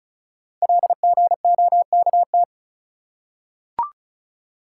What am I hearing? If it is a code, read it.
Morse code, heard as a keyed tone, about 36 wpm, 690 Hz. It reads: LGOKT
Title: Code Groups - 5 Letters - 35wpm